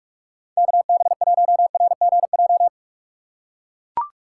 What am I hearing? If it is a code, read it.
Morse code, heard as a keyed tone, about 44 wpm, 700 Hz. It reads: KB1RGJ
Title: USA Call Signs - 6 Characters - 2x Character Spacing - 45wpm